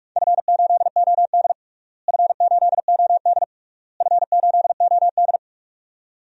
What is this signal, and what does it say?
Morse code, heard as a keyed tone, about 45 wpm, 690 Hz. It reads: F8OD F8OD F8OD
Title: DXCC-100 Call Signs - 4 Characters - Repeated 3x - 45wpm